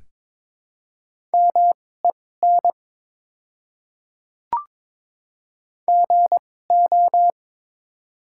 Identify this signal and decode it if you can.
Morse code, heard as a keyed tone, about 21 wpm, spaced as for 16 wpm, 700 Hz. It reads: MEN GO
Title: 2x Character Spacing - Top 200 Words - 22wpm